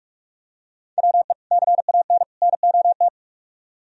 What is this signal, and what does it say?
Morse code, heard as a keyed tone, about 44 wpm, 700 Hz. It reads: WE CAN NOT